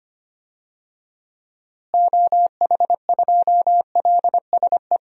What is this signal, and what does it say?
Morse code, heard as a keyed tone, about 25 wpm, 700 Hz. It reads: OH2LSE
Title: DXCC-100 Call Signs - 6 Characters - Repeated 3x - 25wpm